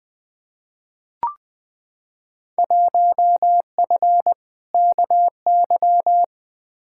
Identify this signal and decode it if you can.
Morse code, heard as a keyed tone, about 20 wpm, 700 Hz. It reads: 1F KY
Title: Field Day Exchanges Encore - 20wpm